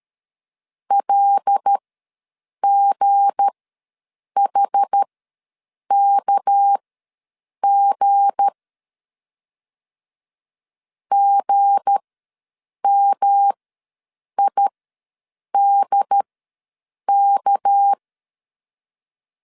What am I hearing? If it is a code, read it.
Morse code, heard as a keyed tone, about 13 wpm, spaced as for 7 wpm, 800 Hz. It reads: LGHKG GMIDK
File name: LEZIONI / 7 GF9Z / 7 G ws 12wpm 3140